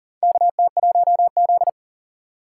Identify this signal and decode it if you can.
Morse code, heard as a keyed tone, about 40 wpm, 700 Hz. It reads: KT1Z
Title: USA Call Signs - 4 Characters - 4x Character Spacing - 40wpm